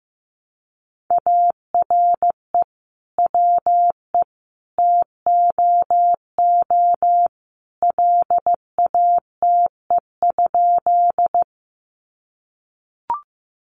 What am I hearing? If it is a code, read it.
Morse code, heard as a keyed tone, about 15 wpm, 700 Hz. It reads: ARE WE TOO LATE?
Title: Sentences from Top 300 Words 15wpm